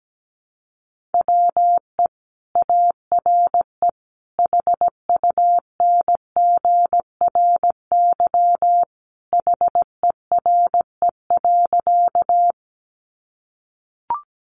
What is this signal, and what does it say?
Morse code, heard as a keyed tone, about 17 wpm, 700 Hz. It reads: WE ARE HUNGRY HERE.
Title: Sentences from Top 1400 Words 17wpm